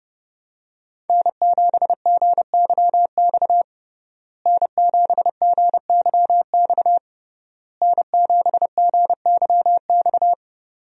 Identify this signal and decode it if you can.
Morse code, heard as a keyed tone, about 30 wpm, 700 Hz. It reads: N7GYX N7GYX N7GYX
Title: USA Call Signs - 5 Characters - Repeated 3x - 30wpm